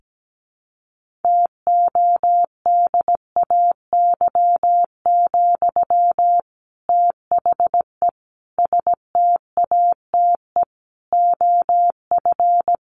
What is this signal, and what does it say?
Morse code, heard as a keyed tone, about 17 wpm, 700 Hz. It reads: TODAY, THE STATE OF
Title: Sentences from Top 1400 Words 17wpm